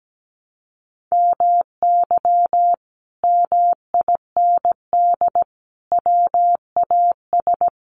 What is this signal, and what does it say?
Morse code, heard as a keyed tone, about 17 wpm, 700 Hz. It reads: MY MIND WAS
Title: Sentences from Top 700 Words 17wpm